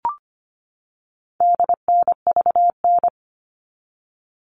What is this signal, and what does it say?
Morse code, heard as a keyed tone, about 25 wpm, 700 Hz. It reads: DNVN
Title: All Cut Numbers - 4-Digit 25wpm